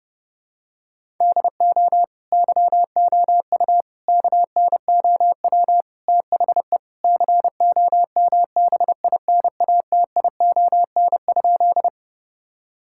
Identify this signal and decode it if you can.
Morse code, heard as a keyed tone, about 30 wpm, 700 Hz. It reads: DO YOU KNOW THE COMBINATION?
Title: Sentences from Top 1000 Words - Review Part 6 - 30wpm